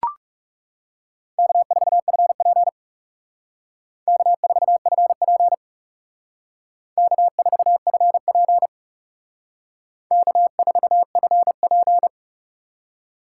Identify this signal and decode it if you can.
Morse code, heard as a keyed tone, about 36 wpm, 700 Hz. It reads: K4FP K4FP K4FP K4FP
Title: Speed Racing - USA 1x2 Call Signs 30wpm